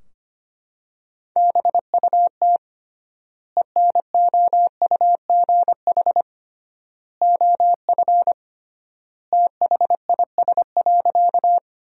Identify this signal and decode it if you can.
Morse code, heard as a keyed tone, about 25 wpm, 700 Hz. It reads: BUT ENOUGH OF THIS.
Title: 3x Word-Spacing - Sentences from Top 400 Words - 25wpm